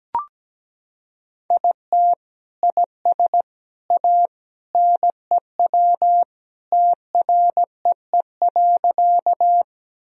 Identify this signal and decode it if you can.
Morse code, heard as a keyed tone, about 17 wpm, 700 Hz. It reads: IT IS A NEW TREE.
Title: Sentences from Top 1000 Words - Review Part 1 - 17wpm